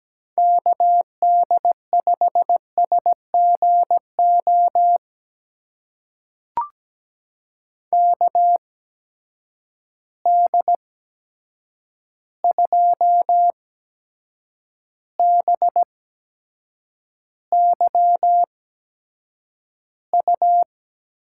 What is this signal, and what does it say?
Morse code, heard as a keyed tone, about 17 wpm, spaced as for 7 wpm, 700 Hz. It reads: KD5SGO KD2BYU
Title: USA Call Signs - 6 Characters - 8x Character Spacing - 17wpm